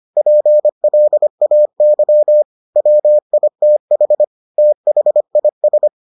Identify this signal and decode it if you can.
Morse code, heard as a keyed tone, about 25 wpm, 600 Hz. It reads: PLAY WITH THIS